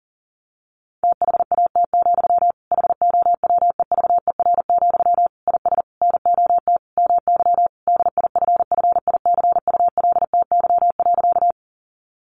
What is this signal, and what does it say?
Morse code, heard as a keyed tone, about 40 wpm, 700 Hz. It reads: THAT, HOWEVER, IS NOT MY DIFFICULTY.
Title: Sentences from Top 1500 Words 40wpm